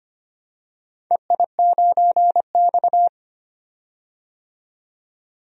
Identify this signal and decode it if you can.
Morse code, heard as a keyed tone, about 25 wpm, 700 Hz. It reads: EI9X
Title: DXCC-100 Call Signs - 4 Characters - 25wpm